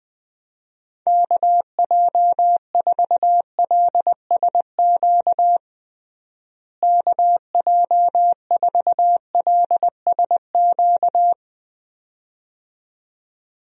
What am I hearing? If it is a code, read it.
Morse code, heard as a keyed tone, about 20 wpm, 700 Hz. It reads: KJ4LSQ KJ4LSQ
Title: USA Call Signs - 6 Characters - Repeated 2x - 20wpm